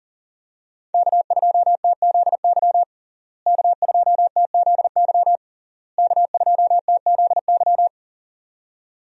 Morse code, heard as a keyed tone, about 37 wpm, 700 Hz. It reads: K2TZY K2TZY K2TZY
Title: USA Call Signs - 5 Characters - Repeated 3x - 40wpm